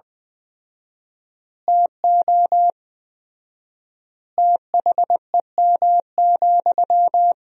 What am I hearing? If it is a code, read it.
Morse code, heard as a keyed tone, about 21 wpm, 700 Hz. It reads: TO THEM,
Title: 4x Word-Spacing - Sentences from Top 400 Words - 20wpm